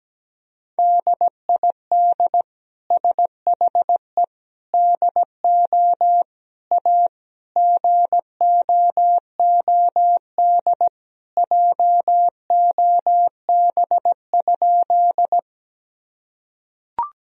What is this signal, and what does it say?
Morse code, heard as a keyed tone, about 17 wpm, 700 Hz. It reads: DID SHE DO A GOOD JOB?
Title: Sentences from Top 1000 Words - Review Part 3 - 17wpm